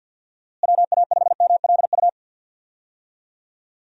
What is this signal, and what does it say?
Morse code, heard as a keyed tone, about 52 wpm, 700 Hz. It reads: WA5MLU
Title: USA Call Signs - 6 Characters - Repeated 3x - 50wpm